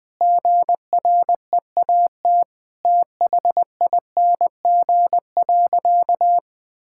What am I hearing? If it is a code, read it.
Morse code, heard as a keyed tone, about 21 wpm, 700 Hz. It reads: GREAT THING.
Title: Sentences from Top 1000 Words - Review Part 5 - 20wpm